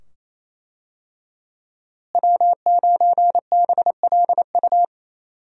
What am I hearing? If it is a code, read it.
Morse code, heard as a keyed tone, about 28 wpm, 700 Hz. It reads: W9BLU
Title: USA Call Signs - 5 Characters - Repeated 2x - 28wpm